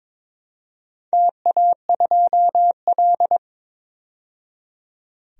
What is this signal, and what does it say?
Morse code, heard as a keyed tone, about 23 wpm, 700 Hz. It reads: TA2L
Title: DXCC-100 Call Signs - 4 Characters - 22wpm